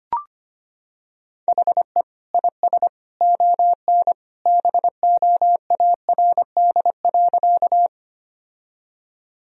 Morse code, heard as a keyed tone, about 25 wpm, 700 Hz. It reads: HE IS ON BOARD.
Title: Sentences from Top 1000 Words - Review Part 4 - 25wpm